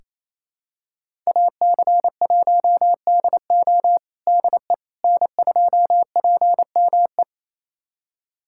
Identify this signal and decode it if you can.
Morse code, heard as a keyed tone, about 28 wpm, 700 Hz. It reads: AC1DO DE N2PME
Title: USA Callsign DE Callsign - 5 Characters - 28wpm